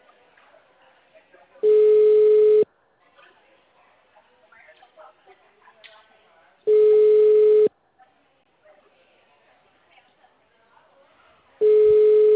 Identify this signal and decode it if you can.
Ringback tone (Europe, 425 Hz), with speech.